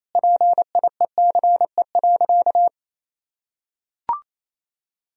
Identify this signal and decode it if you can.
Morse code, heard as a keyed tone, about 28 wpm, 700 Hz. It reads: PIECE.
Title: Sentences from Top 1000 Words - Review Part 3 - 28wpm